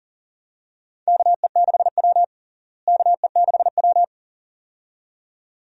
Morse code, heard as a keyed tone, about 37 wpm, 700 Hz. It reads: KE6W KE6W
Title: USA Call Signs - 4 Characters - Repeated 2x - 40wpm